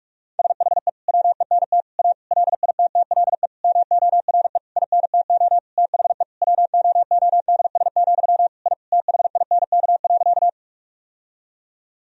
Morse code, heard as a keyed tone, about 46 wpm, 700 Hz. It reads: SHE WENT A LITTLE MORE INTO THE WOODS, I THINK.
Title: Sentences from Top 1000 Words - Review Part 2 - 45wpm